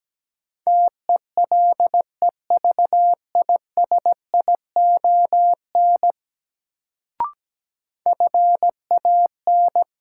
Morse code, heard as a keyed tone, about 17 wpm, 700 Hz. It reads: TELEVISION FAN
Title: Sentences from Top 1000 Words - Review Part 4 - New Word Drill - 17wpm